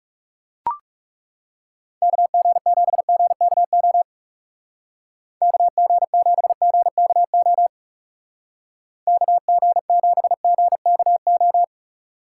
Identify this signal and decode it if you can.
Morse code, heard as a keyed tone, about 35 wpm, 700 Hz. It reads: KG7GKO KG7GKO KG7GKO
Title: Speed Racing - USA Call Signs 30wpm